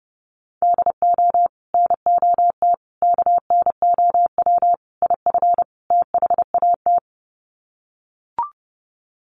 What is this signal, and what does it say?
Morse code, heard as a keyed tone, about 30 wpm, 700 Hz. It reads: DO NOT KNOW IF THAT
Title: Sets of 5 Words Encore2 30wpm